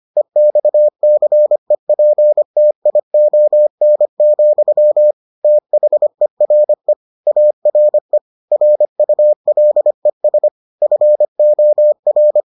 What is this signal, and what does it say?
Morse code, heard as a keyed tone, about 25 wpm, 600 Hz. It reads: EXCEPTION, THERE ARE RULES FOR